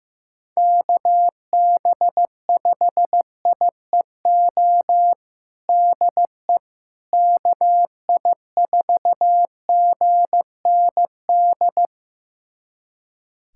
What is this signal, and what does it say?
Morse code, heard as a keyed tone, about 15 wpm, 700 Hz. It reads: KB5IEO DE KI4GND
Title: USA Callsign DE Callsign 15wpm